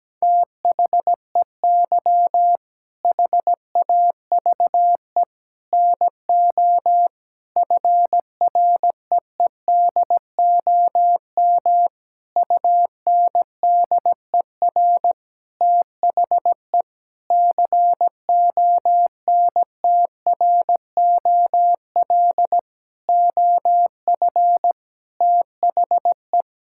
Morse code, heard as a keyed tone, about 17 wpm, 700 Hz. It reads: THEY HAVE NO FREEDOM UNDER THE CONTROL OF THE
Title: Sentences from Top 1000 Words - Review Part 5 - 17wpm